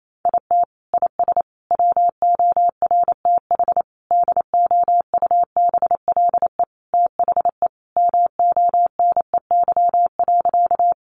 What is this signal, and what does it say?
Morse code, heard as a keyed tone, about 28 wpm, 700 Hz. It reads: IT IS WORTH DOUBLE THE MONEY.
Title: Sentences from Top 1400 Words 28wpm